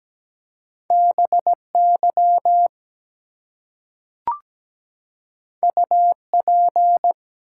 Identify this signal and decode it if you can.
Morse code, heard as a keyed tone, about 17 wpm, 700 Hz. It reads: BY UP
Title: Two Letter Words - 17wpm